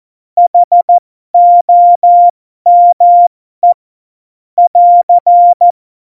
Morse code, heard as a keyed tone, about 14 wpm, spaced as for 12 wpm, 700 Hz. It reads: HOME <AR>
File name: Morsekurs / Witze